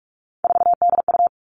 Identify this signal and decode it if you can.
Morse code, heard as a keyed tone, about 44 wpm, 700 Hz. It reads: 4DU
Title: All Cut Numbers - 3-Digit 45wpm